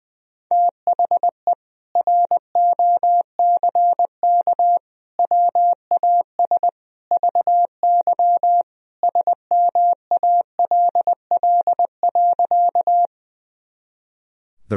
Morse code, heard as a keyed tone, about 21 wpm, 700 Hz. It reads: THE ROCK WAS VY SMALL.
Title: Sentences from Top 400 Words and Abbreviations 20wpm